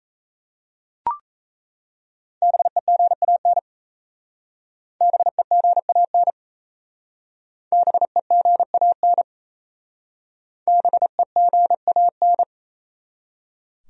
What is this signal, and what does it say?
Morse code, heard as a keyed tone, about 37 wpm, 700 Hz. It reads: BEGAN BEGAN BEGAN BEGAN